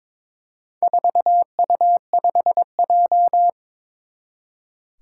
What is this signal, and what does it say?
Morse code, heard as a keyed tone, about 22 wpm, 700 Hz. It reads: 4U5J
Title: Four Challenging Letter-Number Combinations 22wpm